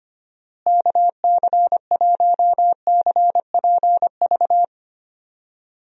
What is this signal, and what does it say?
Morse code, heard as a keyed tone, about 25 wpm, 700 Hz. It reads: KC1CPV
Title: USA Call Signs - 6 Characters - Repeated 3x - 25wpm